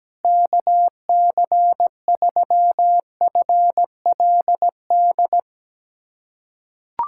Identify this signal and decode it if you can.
Morse code, heard as a keyed tone, about 17 wpm, 700 Hz. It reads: KC3FLD
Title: USA Callsign DE Callsign - 6 Characters - 17wpm